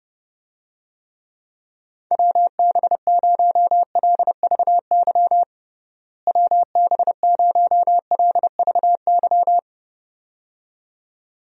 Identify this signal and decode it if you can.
Morse code, heard as a keyed tone, about 32 wpm, 700 Hz. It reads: WB0LVY WB0LVY
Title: USA Call Signs - 6 Characters - Repeated 2x - 30wpm